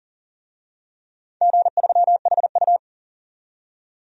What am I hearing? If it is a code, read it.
Morse code, heard as a keyed tone, about 40 wpm, 700 Hz. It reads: G3HU
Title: DXCC-100 Call Signs - 4 Characters - 40wpm